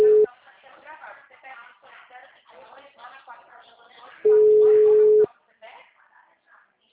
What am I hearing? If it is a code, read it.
Ringback tone (Europe, 425 Hz), with speech.